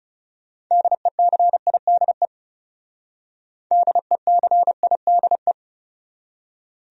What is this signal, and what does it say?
Morse code, heard as a keyed tone, about 37 wpm, 700 Hz. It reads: DECIDE DECIDE
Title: Speed-Racing Top 500 Words 30wpm